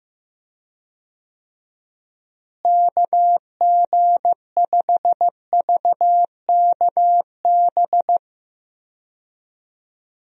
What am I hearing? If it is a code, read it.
Morse code, heard as a keyed tone, about 15 wpm, 700 Hz. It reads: KG5VKB
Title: USA Call Signs - 6 Characters - Repeated 3x - 15wpm